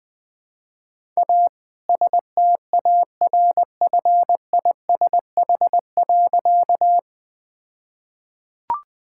Morse code, heard as a keyed tone, about 20 wpm, 700 Hz. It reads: A STARFISH.
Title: All-Age Appropriate Jokes - 20wpm